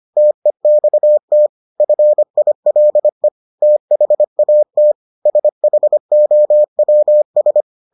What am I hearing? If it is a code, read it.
Morse code, heard as a keyed tone, about 25 wpm, 600 Hz. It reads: TEXT FILE THAT SHOWS